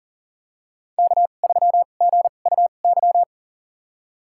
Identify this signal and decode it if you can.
Morse code, heard as a keyed tone, about 40 wpm, spaced as for 29 wpm, 700 Hz. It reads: K3GUY
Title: USA Call Signs - 5 Characters - 2x Character Spacing - 40wpm